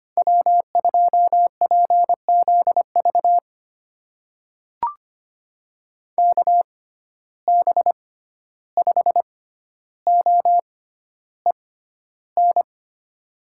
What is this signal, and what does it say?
Morse code, heard as a keyed tone, about 25 wpm, spaced as for 12 wpm, 700 Hz. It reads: W2PZV KB5OEN
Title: USA Call Signs - 6x Character Spacing - 25wpm